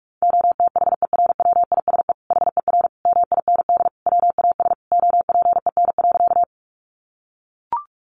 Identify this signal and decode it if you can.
Morse code, heard as a keyed tone, about 44 wpm, 700 Hz. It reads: OTHERWISE HER MIND WAS OPEN.
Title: Sentences from Top 1200 Words 45wpm